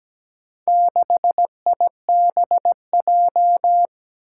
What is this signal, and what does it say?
Morse code, heard as a keyed tone, about 17 wpm, 700 Hz. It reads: 6IBJ
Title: Four Challenging Letter-Number Combinations 17wpm